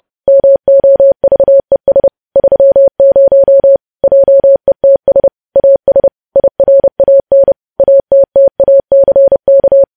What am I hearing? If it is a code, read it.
Morse code, heard as a keyed tone, about 30 wpm, 550 Hz. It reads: MOVES 30 JETS AS IRAN ATTACK
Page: news headlines in morse code at 30 wpm